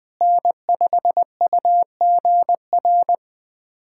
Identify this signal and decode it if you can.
Morse code, heard as a keyed tone, about 21 wpm, 700 Hz. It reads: N5UGR
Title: USA Callsign DE Callsign - 5 Characters - 20wpm